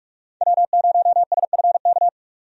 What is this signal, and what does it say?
Morse code, heard as a keyed tone, about 41 wpm, 700 Hz. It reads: W0SFK
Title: USA Call Signs - 5 Characters - 45wpm